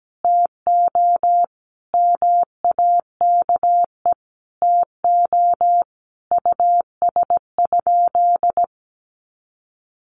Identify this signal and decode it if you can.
Morse code, heard as a keyed tone, about 17 wpm, 700 Hz. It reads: TO MAKE TO US?
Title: Sentences from Top 1400 Words 17wpm